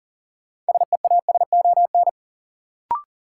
Morse code, heard as a keyed tone, about 40 wpm, 700 Hz. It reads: SEASON
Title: Sentences from Top 1400 Words - New Word Drill - 40wpm